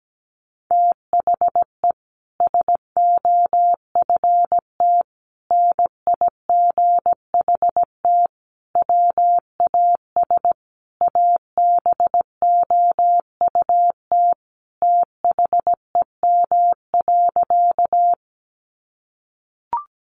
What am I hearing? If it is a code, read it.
Morse code, heard as a keyed tone, about 17 wpm, 700 Hz. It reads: THE SOFT NIGHT WAS ABOUT THEM.
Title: Sentences from Top 1500 Words 17wpm